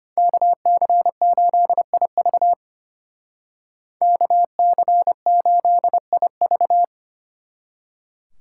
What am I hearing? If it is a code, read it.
Morse code, heard as a keyed tone, about 32 wpm, 700 Hz. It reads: KC8IV KC8IV